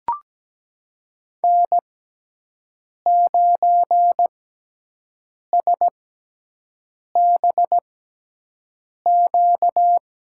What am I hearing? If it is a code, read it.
Morse code, heard as a keyed tone, about 17 wpm, spaced as for 6 wpm, 700 Hz. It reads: N9SBQ
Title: USA Call Signs - 6x Character Spacing - 17wpm